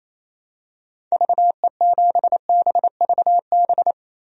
Morse code, heard as a keyed tone, about 28 wpm, 700 Hz. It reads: VE7BVB
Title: DXCC-100 Call Signs - 6 Characters - 28wpm